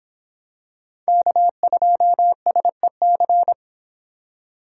Morse code, heard as a keyed tone, about 26 wpm, 700 Hz. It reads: K2SEC